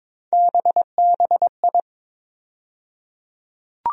Morse code, heard as a keyed tone, about 23 wpm, 700 Hz. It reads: BBI
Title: USA Call Sign Postfixes - 3 Letters 22wpm